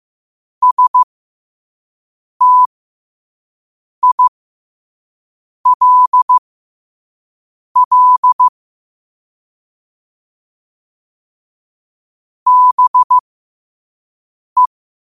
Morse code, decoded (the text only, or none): STILL BE